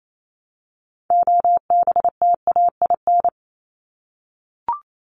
Morse code, heard as a keyed tone, about 28 wpm, 700 Hz. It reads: OBTAIN
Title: ED Suffix - Top 100 Words with Base Words 28wpm